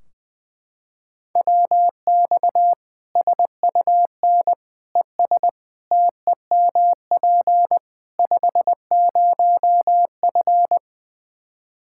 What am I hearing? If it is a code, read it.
Morse code, heard as a keyed tone, about 20 wpm, 700 Hz. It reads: WX SUN ES TEMP 50F
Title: Practice QSO - Exchange 2 - Info - 20wpm